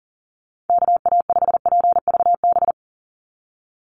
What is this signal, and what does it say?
Morse code, heard as a keyed tone, about 40 wpm, 700 Hz. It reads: KA5PVB
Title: NA SOTA Chaser Callsigns 40wpm